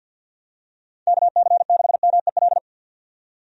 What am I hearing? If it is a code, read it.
Morse code, heard as a keyed tone, about 51 wpm, 700 Hz. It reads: KC6MEL
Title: USA Call Signs - 6 Characters - 50wpm